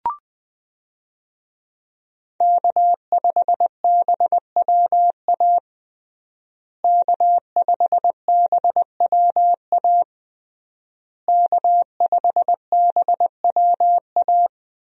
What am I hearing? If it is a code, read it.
Morse code, heard as a keyed tone, about 20 wpm, 700 Hz. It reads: K5BWA K5BWA K5BWA
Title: USA Call Signs - 5 Characters - Repeated 3x - 20wpm